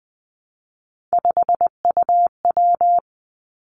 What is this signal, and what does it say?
Morse code, heard as a keyed tone, about 20 wpm, 700 Hz. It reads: 5UW
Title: Three Challenging Letter-Number Combinations 20wpm